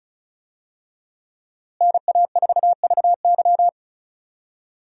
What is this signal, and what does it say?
Morse code, heard as a keyed tone, about 36 wpm, 700 Hz. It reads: NA4VY
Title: USA Call Signs - 5 Characters - Repeated 2x - 35wpm